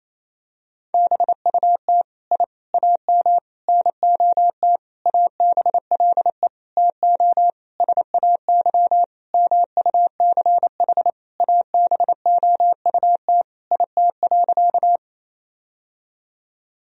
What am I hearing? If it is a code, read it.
Morse code, heard as a keyed tone, about 28 wpm, 700 Hz. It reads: BUT I AM NOT ABLE TO SAY MUCH ABOUT IT.
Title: Sentences from Top 1000 Words - Review Part 2 - 28wpm